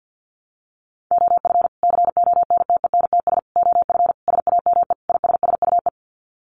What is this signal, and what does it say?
Morse code, heard as a keyed tone, about 50 wpm, 700 Hz. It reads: OF CONTENTS OF SAME ISSUE
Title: Sets of 5 Words Encore2 50wpm